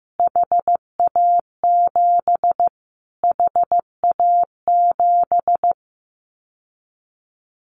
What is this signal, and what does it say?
Morse code, heard as a keyed tone, about 15 wpm, 700 Hz. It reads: HA7 HA7
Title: Callsign Prefixes for Top 25 DXCC - Repeated 3x - 15wpm